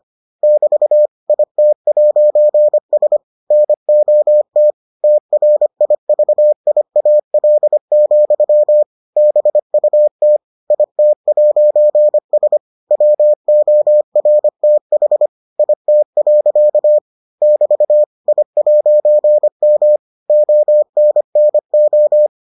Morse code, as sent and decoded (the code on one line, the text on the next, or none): -...- / .. - .----. ... / -. --- - / - .-. .. ...- .. .- .-.. --..-- / -... ..- - / .. - .----. ... / .-- --- .-. - .... / .. - .-.-.- / -...- / .. .----. -- / --- -. -. ---
<BT> IT'S NOT TRIVIAL, BUT IT'S WORTH IT. <BT> I'M ONNO